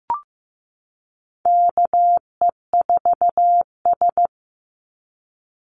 Morse code, heard as a keyed tone, about 15 wpm, 700 Hz. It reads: KE4S
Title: Call Signs Repeated 3x - 15wpm